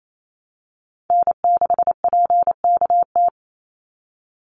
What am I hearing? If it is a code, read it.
Morse code, heard as a keyed tone, about 28 wpm, 700 Hz. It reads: N6PKT
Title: NA SOTA Chaser Callsigns 28wpm